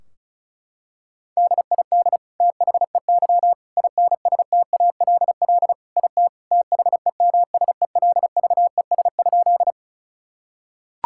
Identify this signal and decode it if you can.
Morse code, heard as a keyed tone, about 36 wpm, 700 Hz. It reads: DID THEY INSTALL IT THEMSELVES?
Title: Sentences from Top 1600 Words 35wpm